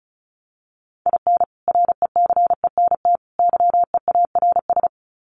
Morse code, heard as a keyed tone, about 34 wpm, 700 Hz. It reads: IN RECENT YEARS
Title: Sets of 3 Words 35wpm